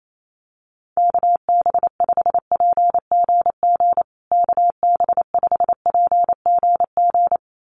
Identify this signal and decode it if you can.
Morse code, heard as a keyed tone, about 28 wpm, 700 Hz. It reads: KB5PGG KB5PGG
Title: USA Call Signs Repeated 2x 28wpm